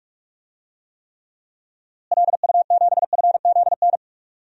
Morse code, heard as a keyed tone, about 46 wpm, 700 Hz. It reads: LU7FZN
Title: DXCC-100 Call Signs - 6 Characters - Repeated 2x - 45wpm